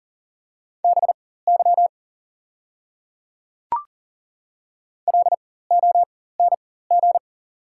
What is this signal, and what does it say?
Morse code, heard as a keyed tone, about 40 wpm, spaced as for 19 wpm, 690 Hz. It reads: BY LONG
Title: 4x Character Spacing - Top 200 Words - 40wpm